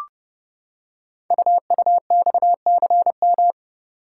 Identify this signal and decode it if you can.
Morse code, heard as a keyed tone, about 30 wpm, 700 Hz. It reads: UUXCM
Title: Code Groups - 5 Letters - 30wpm